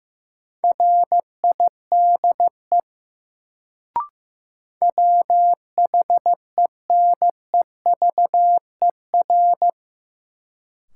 Morse code, heard as a keyed tone, about 15 wpm, 700 Hz. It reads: RIDE WHENEVER
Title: Sentences from Top 1200 Words - New Word Drill Alt - 15wpm